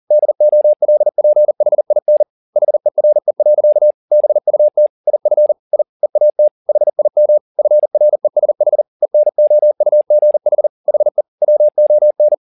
Morse code, heard as a keyed tone, about 40 wpm, 600 Hz. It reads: DOLPHIN HERE. BUT IF I EAT HIM FRESH ENOUGH HE WON